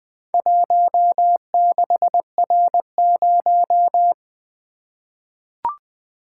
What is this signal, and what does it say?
Morse code, heard as a keyed tone, about 20 wpm, 700 Hz. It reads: 16R0
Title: Four Letter-Number Combinations - 20wpm